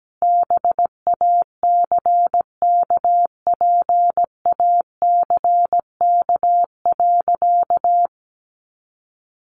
Morse code, heard as a keyed tone, about 17 wpm, 700 Hz. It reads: BACKPACK.
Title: Additional Easy Phrases 17wpm